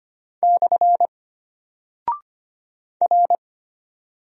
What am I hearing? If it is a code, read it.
Morse code, heard as a keyed tone, about 25 wpm, 700 Hz. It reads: / R
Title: Single Character (Letter:Number:Punctuation:Prosign) Alt 25wpm